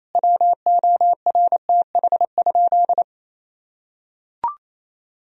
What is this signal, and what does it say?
Morse code, heard as a keyed tone, about 28 wpm, 700 Hz. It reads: WORTH?
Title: Sentences from Top 1000 Words - Review Part 4 - 28wpm